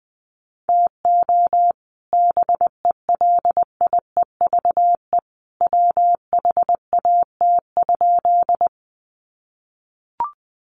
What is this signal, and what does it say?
Morse code, heard as a keyed tone, about 20 wpm, 700 Hz. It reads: TO BELIEVE WHAT?
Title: Sentences from Top 1000 Words 20wpm